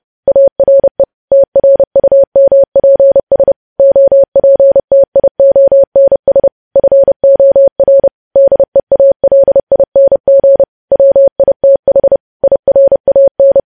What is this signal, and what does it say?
Morse code, heard as a keyed tone, about 30 wpm, 550 Hz. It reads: ARE TRUMPS OPTIONS FOR DEALING WITH IRAN
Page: news headlines in morse code at 30 wpm